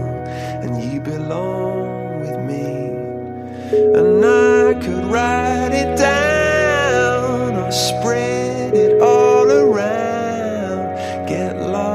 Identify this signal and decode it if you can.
Ringback tone (Europe, 425 Hz), with noise.